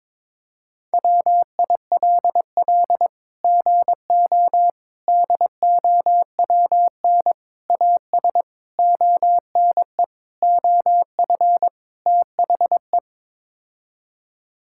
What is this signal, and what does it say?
Morse code, heard as a keyed tone, about 22 wpm, 700 Hz. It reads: WILL GO DOWN AS ONE OF THE
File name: practice / 22wpm / Sets of 7 Words - 22wpm